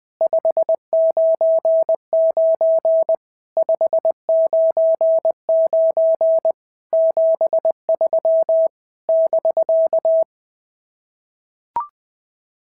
Morse code, heard as a keyed tone, about 20 wpm, 650 Hz. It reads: 599 599 73 <BK>
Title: SOTA QSO Practice 20wpm